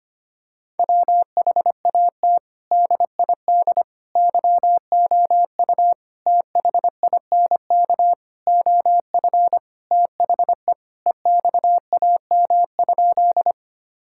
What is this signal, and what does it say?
Morse code, heard as a keyed tone, about 25 wpm, 700 Hz. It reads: WHAT DID YOU THINK OF THE EXAM?
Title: Sentences from Top 1000 Words - Review Part 5 - 25wpm